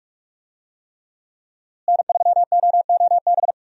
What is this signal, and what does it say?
Morse code, heard as a keyed tone, about 43 wpm, 700 Hz. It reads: N3OOB
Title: USA Call Signs - 5 Characters - Repeated 3x - 45wpm